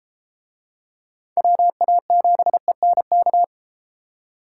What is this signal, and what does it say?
Morse code, heard as a keyed tone, about 31 wpm, 700 Hz. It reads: WA7ENK